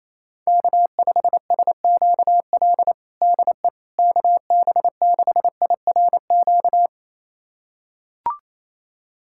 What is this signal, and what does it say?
Morse code, heard as a keyed tone, about 28 wpm, 700 Hz. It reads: K5SQL DE KB6IRQ
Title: USA Callsign DE Callsign 28wpm